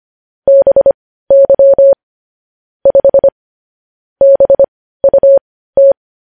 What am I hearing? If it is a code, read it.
Morse code, heard as a keyed tone, about 25 wpm, spaced as for 15 wpm, 550 Hz. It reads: BY 5 BUT
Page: news headlines in morse code at 15 wpm